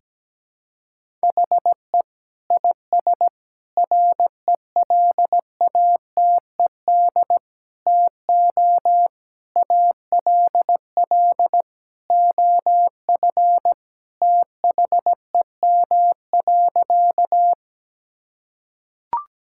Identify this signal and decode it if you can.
Morse code, heard as a keyed tone, about 17 wpm, 700 Hz. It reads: HE IS RELATED TO ALL OF THEM.
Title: Sentences from Top 1000 Words - Review Part 5 - 17wpm